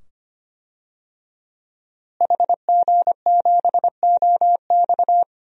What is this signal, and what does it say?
Morse code, heard as a keyed tone, about 25 wpm, 700 Hz. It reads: HG7OX
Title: DXCC-100 Call Signs - 5 Characters - Repeated 2x - 25wpm